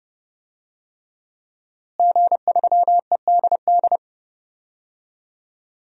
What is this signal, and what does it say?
Morse code, heard as a keyed tone, about 28 wpm, 700 Hz. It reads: G3EDD
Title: DXCC-100 Call Signs - 5 Characters - Repeated 3x - 30wpm